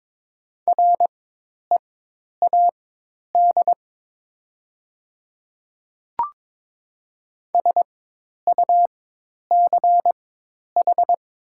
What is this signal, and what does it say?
Morse code, heard as a keyed tone, about 22 wpm, spaced as for 10 wpm, 700 Hz. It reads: READ SUCH
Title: 4x Character Spacing - Top 200 Words - 22wpm